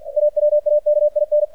Morse code, heard as a keyed tone, about 50 wpm, 600 Hz, with radio noise. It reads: AWAKIN
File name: archived / CRSnet